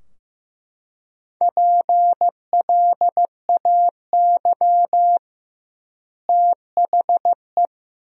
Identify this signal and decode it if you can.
Morse code, heard as a keyed tone, about 14 wpm, 700 Hz. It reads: PLAY THE
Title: 2x Word-Spacing - Sentences from Top 500 Words - 15wpm